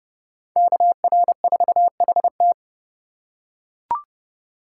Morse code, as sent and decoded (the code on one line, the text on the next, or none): -.- .-. ....- .... -
KR4HT